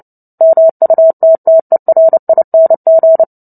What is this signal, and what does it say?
Morse code, heard as a keyed tone, about 30 wpm, 650 Hz. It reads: MUTTERING
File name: Book Two / 03-Chapter 3 - 29 WPM